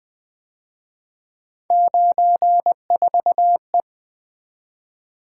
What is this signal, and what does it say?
Morse code, heard as a keyed tone, about 23 wpm, 700 Hz. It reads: 94E
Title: Speed-Racing - Three Letter-Number Combinations - 20wpm